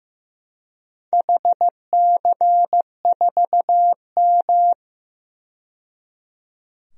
Morse code, heard as a keyed tone, about 15 wpm, 700 Hz. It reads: HC4M
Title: DXCC-100 Call Signs - 4 Characters - 15wpm